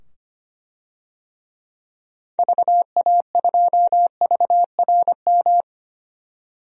Morse code, heard as a keyed tone, about 25 wpm, 700 Hz. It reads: VA2VRM
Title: DXCC-100 Call Signs - 6 Characters - 25wpm